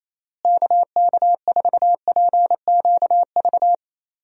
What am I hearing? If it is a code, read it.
Morse code, heard as a keyed tone, about 28 wpm, 700 Hz. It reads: KK4PQV
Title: USA Call Signs - 6 Characters - Repeated 2x - 28wpm